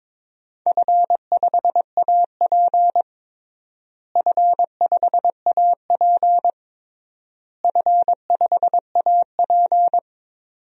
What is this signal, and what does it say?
Morse code, heard as a keyed tone, about 22 wpm, 700 Hz. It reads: F5AP F5AP F5AP
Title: DXCC-100 Call Signs - 4 Characters - Repeated 3x - 22wpm